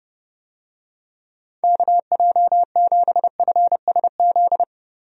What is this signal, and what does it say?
Morse code, heard as a keyed tone, about 30 wpm, 700 Hz. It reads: KJ7FSZ
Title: USA Call Signs - 6 Characters - Repeated 2x - 30wpm